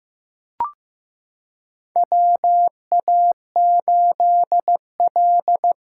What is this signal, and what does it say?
Morse code, heard as a keyed tone, about 15 wpm, 700 Hz. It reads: WA8L
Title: USA Callsign DE Callsign 15wpm